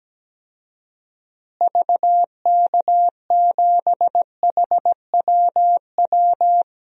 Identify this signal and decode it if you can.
Morse code, heard as a keyed tone, about 17 wpm, 700 Hz. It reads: VK7HWW
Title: DXCC-100 Call Signs - 6 Characters - Repeated 3x - 17wpm